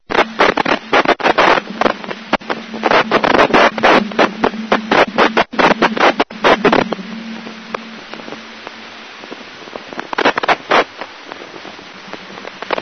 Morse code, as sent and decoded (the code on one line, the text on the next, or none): none